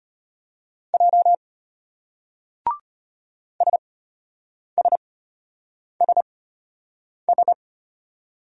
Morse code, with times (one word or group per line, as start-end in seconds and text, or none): 0.9-1.3 J
3.6-7.5 SSSS